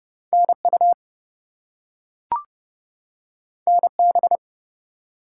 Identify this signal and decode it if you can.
Morse code, heard as a keyed tone, about 30 wpm, 700 Hz. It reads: NU NB
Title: US States and Canadian Provinces 30wpm